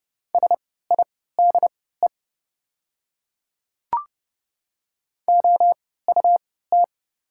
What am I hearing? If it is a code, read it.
Morse code, heard as a keyed tone, about 30 wpm, spaced as for 17 wpm, 700 Hz. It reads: SIDE OUT
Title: 3x Character Spacing - Top 100 Words - 30wpm